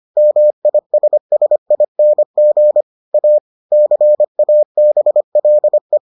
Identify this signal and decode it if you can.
Morse code, heard as a keyed tone, about 25 wpm, 600 Hz. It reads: MISSING A CABLE